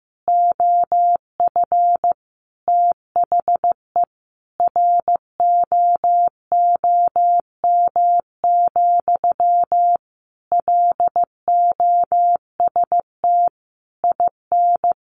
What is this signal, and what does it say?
Morse code, heard as a keyed tone, about 14 wpm, 700 Hz. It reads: OF THE ROOM, LOST IN
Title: Sentences from Top 1400 Words 15wpm